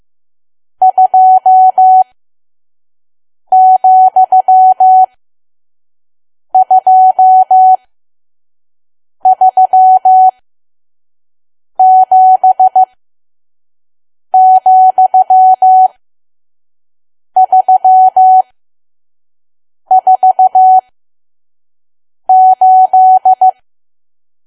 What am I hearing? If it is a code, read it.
Morse code, heard as a keyed tone, about 15 wpm, spaced as for 5 wpm, 750 Hz. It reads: 2,237,348
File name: Archive / 5 WPM files / PM